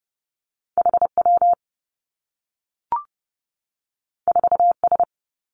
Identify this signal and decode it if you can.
Morse code, heard as a keyed tone, about 30 wpm, 700 Hz. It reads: H W 4 S
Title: Two Challenging Letter-Number Combinations 30wpm